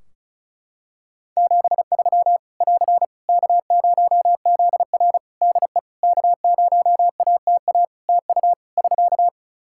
Morse code, heard as a keyed tone, about 36 wpm, 700 Hz. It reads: 73 <AR> K0ZR DE K0ATA TU <SK>
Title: Practice QSO - Exchange 4 - Ending - 35wpm